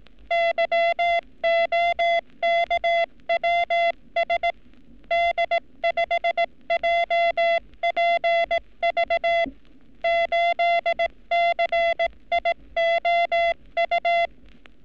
Morse code, heard as a keyed tone, about 18 wpm, 680 Hz. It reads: YOKWS D5JPV 8CIOU
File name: training / NAVPERS 11224-11235 / NAVPERS 11233 RA RB